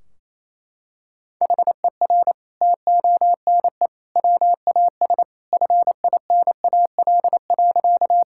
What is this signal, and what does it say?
Morse code, heard as a keyed tone, about 27 wpm, 700 Hz. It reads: HER TONE WAS FINAL.
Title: Sentences from Top 1600 Words 28wpm